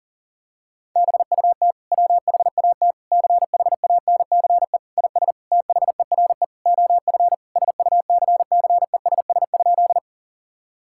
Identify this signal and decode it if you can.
Morse code, heard as a keyed tone, about 40 wpm, 700 Hz. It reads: BUT WHAT CHANCE IS THERE OF SUCCESS?
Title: Sentences from Top 1000 Words - Review Part 6 - 40wpm